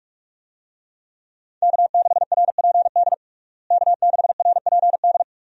Morse code, heard as a keyed tone, about 46 wpm, 700 Hz. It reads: K6RPD K6RPD
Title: USA Call Signs - 5 Characters - Repeated 2x - 45wpm